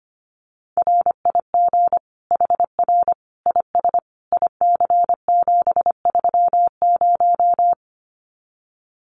Morse code, heard as a keyed tone, about 25 wpm, 700 Hz. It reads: RIG HR IS IC730
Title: Common QSO Phrases 25wpm